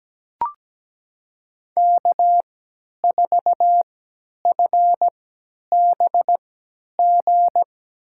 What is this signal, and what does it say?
Morse code, heard as a keyed tone, about 17 wpm, spaced as for 10 wpm, 700 Hz. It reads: K4FBG
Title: USA Call Signs - 5 Characters - 3x Character Spacing - 17wpm